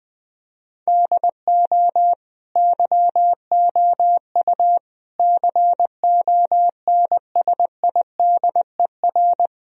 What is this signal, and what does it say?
Morse code, heard as a keyed tone, about 20 wpm, 700 Hz. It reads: DO YOU CONSIDER
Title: Sentences from Top 1000 Words - Review Part 4 - 20wpm